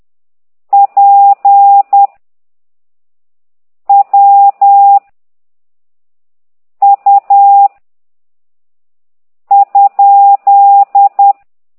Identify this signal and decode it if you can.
Morse code, heard as a keyed tone, about 10 wpm, 800 Hz. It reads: P W U ?
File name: ct2idw / exerc8